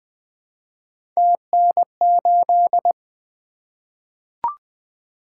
Morse code, heard as a keyed tone, about 20 wpm, 700 Hz. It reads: TN8
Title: Three Letter-Number Combinations - 20wpm